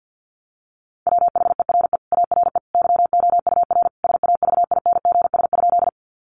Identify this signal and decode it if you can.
Morse code, heard as a keyed tone, about 50 wpm, 700 Hz. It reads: WHERE ARE YOUR SAVINGS?
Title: Sentences from Top 1200 Words 50wpm